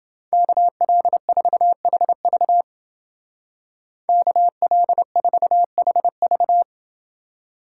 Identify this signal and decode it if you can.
Morse code, heard as a keyed tone, about 29 wpm, 700 Hz. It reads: KL4HV KL4HV